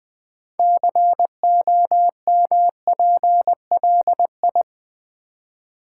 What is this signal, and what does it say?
Morse code, heard as a keyed tone, about 20 wpm, 700 Hz. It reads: COMPLI
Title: Build A Word - 6 Letters At A Time - 12-Letter Words - 20wpm